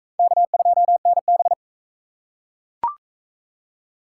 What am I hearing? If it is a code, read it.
Morse code, heard as a keyed tone, about 35 wpm, 700 Hz. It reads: K2NB
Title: Speed Racing - USA 1x2 Call Signs 28wpm